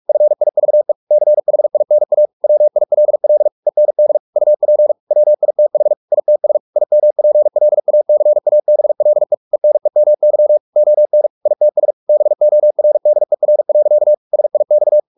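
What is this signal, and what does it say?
Morse code, heard as a keyed tone, about 44 wpm, 600 Hz. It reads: FIVE CHINA WILL END UP WITH ITS IMPLACABLE ENEMY ON ITS BORDER. SIX
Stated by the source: Morse practice at 45 wpm, the novel Never by Ken Follett in English, from CW med Gnister.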